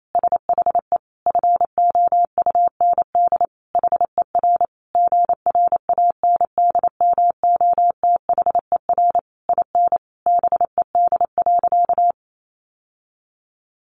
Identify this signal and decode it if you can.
Morse code, heard as a keyed tone, about 28 wpm, 700 Hz. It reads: SHE FOUND HER GRANDMOTHER IN BED.
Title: Sentences from Top 1200 Words 28wpm